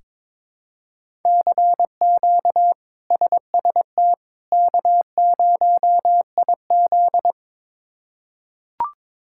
Morse code, heard as a keyed tone, about 22 wpm, 700 Hz. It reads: CQ SST K0IZ
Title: K1USN SST QSO Practice - 22wpm